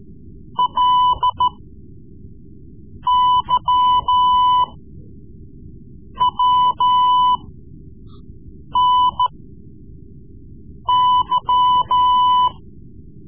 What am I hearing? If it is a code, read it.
Morse code, heard as a keyed tone, about 13 wpm, spaced as for 5 wpm, 1000 Hz. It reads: LYWNY
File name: Morse Lessons / Morse Lessons / Tape-3B